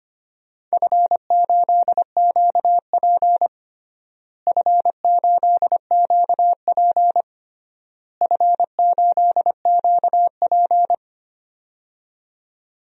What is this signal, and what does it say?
Morse code, heard as a keyed tone, about 25 wpm, 700 Hz. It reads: F8QP F8QP F8QP
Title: DXCC-100 Call Signs - 4 Characters - Repeated 3x - 25wpm